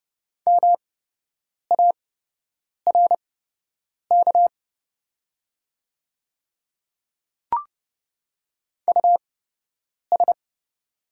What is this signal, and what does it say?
Morse code, heard as a keyed tone, about 30 wpm, spaced as for 8 wpm, 700 Hz. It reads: MARK US
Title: 8x Character Spacing - Top 300 Words - 30wpm